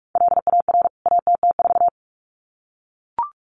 Morse code, heard as a keyed tone, about 44 wpm, 700 Hz. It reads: LAR ATT4
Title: CWT Exchange 45wpm